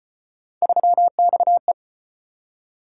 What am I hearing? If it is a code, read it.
Morse code, heard as a keyed tone, about 34 wpm, 700 Hz. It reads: 3XE